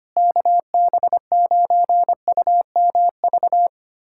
Morse code, heard as a keyed tone, about 25 wpm, 700 Hz. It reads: KB9UMV
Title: USA Callsign DE Callsign 25wpm